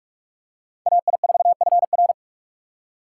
Morse code, heard as a keyed tone, about 45 wpm, 700 Hz. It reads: AI4FR